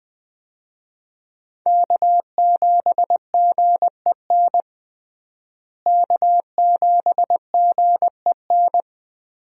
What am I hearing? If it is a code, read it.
Morse code, heard as a keyed tone, about 20 wpm, 700 Hz. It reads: K7GEN K7GEN
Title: USA Call Signs - 5 Characters - Repeated 3x - 20wpm